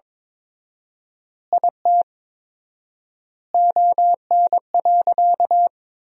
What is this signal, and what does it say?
Morse code, heard as a keyed tone, about 21 wpm, 700 Hz. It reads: IT ON.
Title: 4x Word-Spacing - Sentences from Top 200 Words - 22wpm